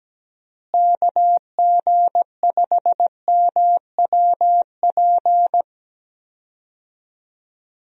Morse code, heard as a keyed tone, about 17 wpm, 700 Hz. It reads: KG5MWP
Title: USA Call Signs - 6 Characters - Repeated 3x - 17wpm